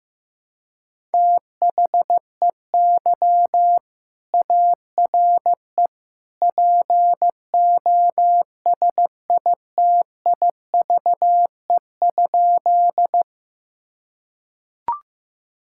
Morse code, heard as a keyed tone, about 15 wpm, 700 Hz. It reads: THEY ARE POSITIVE?
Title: Sentences from Top 1000 Words - Review Part 5 - 15wpm